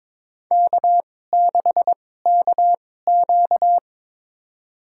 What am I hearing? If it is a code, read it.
Morse code, heard as a keyed tone, about 22 wpm, spaced as for 16 wpm, 700 Hz. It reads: K6KQ
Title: USA Call Signs - 4 Characters - 2x Character Spacing - 22wpm